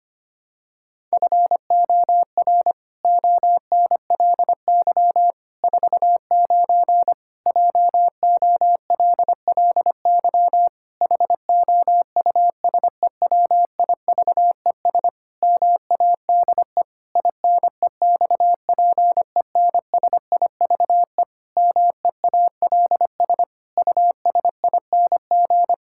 Morse code, heard as a keyed tone, about 25 wpm, 700 Hz. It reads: FOR ONLY 49 JOLLY HOUSEWIVES MADE INEXPENSIVE MEALS USING
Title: Pangram Challenge v2 - 25wpm